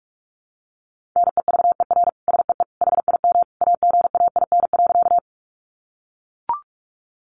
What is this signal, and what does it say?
Morse code, heard as a keyed tone, about 44 wpm, 700 Hz. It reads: NEVER SEE HIM AGAIN.
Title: Sentences from Top 300 Words 45wpm